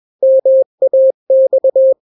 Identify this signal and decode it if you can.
Morse code, heard as a keyed tone, about 21 wpm, 530 Hz. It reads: MAX
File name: daily / 20wpm